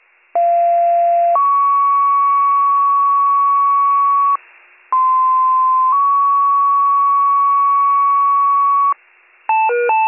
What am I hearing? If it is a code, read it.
Two-tone page, heard as a one-second tone, then a three-second tone, about 690 then 1120 Hz, about 1040 then 1120 Hz. It is heard from 0.3 to 8.9 s.